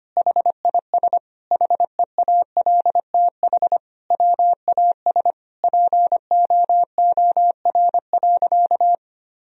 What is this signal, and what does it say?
Morse code, heard as a keyed tone, about 25 wpm, 700 Hz. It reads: HIS HEALTH WAS POOR.
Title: Sentences from Top 1000 Words - Review Part 3 - 25wpm